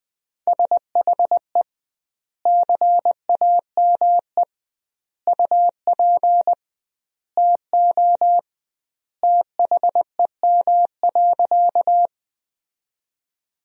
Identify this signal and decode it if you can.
Morse code, heard as a keyed tone, about 21 wpm, 700 Hz. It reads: SHE CAME UP TO THEM.
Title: 2x Word-Spacing - Sentences from Top 200 Words - 20wpm